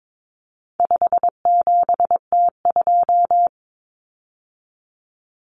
Morse code, heard as a keyed tone, about 22 wpm, 700 Hz. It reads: 57T2
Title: Common Cut Numbers - 4-Digit 22wpm